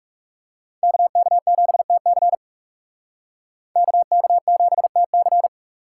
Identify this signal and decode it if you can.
Morse code, heard as a keyed tone, about 36 wpm, 700 Hz. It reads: KK7TC KK7TC
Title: Speed Racing - USA 2x2 Call Signs 30wpm